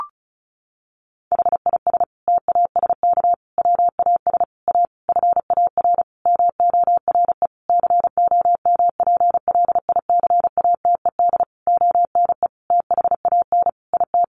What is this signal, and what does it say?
Morse code, heard as a keyed tone, about 34 wpm, 700 Hz. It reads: HIS TASK WAS A FAR MORE COMPLICATED ONE THAN IT
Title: Sentences from Top 1200 Words 35wpm